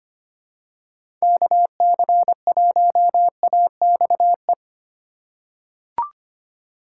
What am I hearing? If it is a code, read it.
Morse code, heard as a keyed tone, about 25 wpm, 700 Hz. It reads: KC1AXE
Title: USA Call Signs - 2x Character Spacing - 25wpm